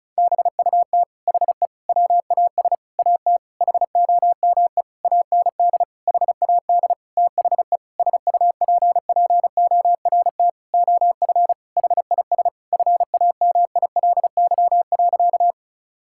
Morse code, heard as a keyed tone, about 36 wpm, 700 Hz. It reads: BUT HE WAS AT HOME AND HAD THE SUPPORT OF HIS FAMILY.
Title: Sentences from Top 1000 Words - Review Part 5 - 35wpm